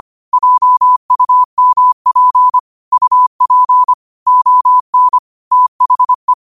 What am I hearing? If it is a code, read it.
Morse code, heard as a keyed tone, about 25 wpm, 1000 Hz. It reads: JUMP UP ON THE
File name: qotdfiles / 25wpm